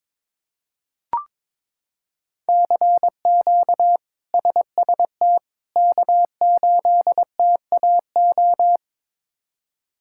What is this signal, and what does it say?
Morse code, heard as a keyed tone, about 22 wpm, 700 Hz. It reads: CQ SST K8TAO
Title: K1USN SST QSO Practice - 22wpm